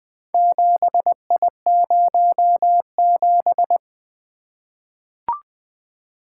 Morse code, heard as a keyed tone, about 20 wpm, 700 Hz. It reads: 7I07
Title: Four Letter-Number Combinations - 20wpm